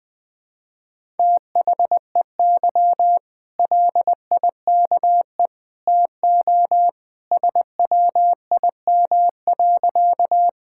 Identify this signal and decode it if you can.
Morse code, heard as a keyed tone, about 20 wpm, 700 Hz. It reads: THEY LIKE TO SWIM.
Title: Easy Sentences with Hints - 20wpm